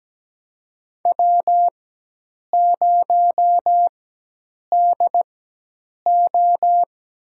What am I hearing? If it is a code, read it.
Morse code, heard as a keyed tone, about 17 wpm, spaced as for 8 wpm, 700 Hz. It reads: W0DO
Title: USA Call Signs - 4 Characters - 4x Character Spacing - 17wpm